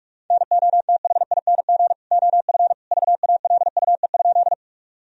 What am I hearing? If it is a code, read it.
Morse code, heard as a keyed tone, about 46 wpm, 700 Hz. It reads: NOTHING OF VALUE?
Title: Sentences from Top 1000 Words - Review Part 3 - 45wpm